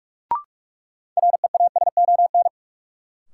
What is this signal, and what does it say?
Morse code, heard as a keyed tone, about 43 wpm, 700 Hz. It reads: REASON
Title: Sentences from Top 700 Words - New Word Drill - 45wpm